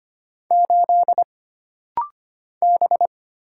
Morse code, heard as a keyed tone, about 25 wpm, 700 Hz. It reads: 8 B
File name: Single Character (Letter:Number:Punctuation:Prosign) - Rapid-Fire - 25wpm